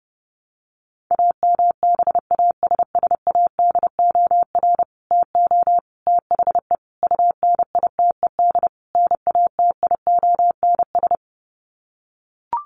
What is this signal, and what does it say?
Morse code, heard as a keyed tone, about 30 wpm, 700 Hz. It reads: AMBASSADOR TO THE UNITED NATIONS
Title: Sets of 5 Words Encore2 30wpm